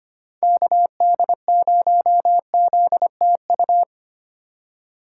Morse code, heard as a keyed tone, about 25 wpm, 700 Hz. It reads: KD0ZTU
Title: USA Call Signs - 6 Characters - 2x Character Spacing - 25wpm